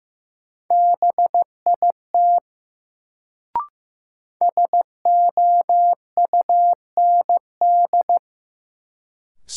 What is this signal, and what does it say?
Morse code, heard as a keyed tone, about 15 wpm, 700 Hz. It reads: BIT SOUND
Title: Sets of 3 Words - New Word Drill - 15wpm